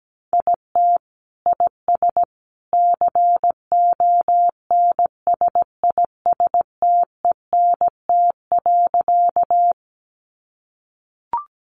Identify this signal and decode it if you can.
Morse code, heard as a keyed tone, about 17 wpm, 700 Hz. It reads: IT IS CONSISTENT.
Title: Sentences from Top 1500 Words 17wpm